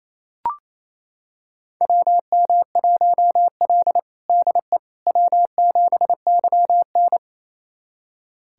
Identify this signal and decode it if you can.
Morse code, heard as a keyed tone, about 27 wpm, 690 Hz. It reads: WM1L DE W7YN
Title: USA Callsign DE Callsign - 4 Characters - 28wpm